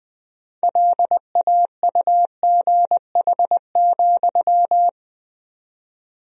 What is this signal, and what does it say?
Morse code, heard as a keyed tone, about 20 wpm, 700 Hz. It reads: LAUGH,
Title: 4x Word-Spacing - Sentences from Top 500 Words - 20wpm